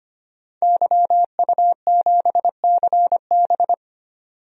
Morse code, heard as a keyed tone, about 25 wpm, 700 Hz. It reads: YU7CB
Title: DXCC-100 Call Signs - 5 Characters - 25wpm